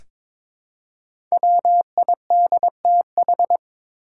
Morse code, heard as a keyed tone, about 22 wpm, 700 Hz. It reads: WIDTH
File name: 5 Letter Words - First 3 Letters Given - 22wpm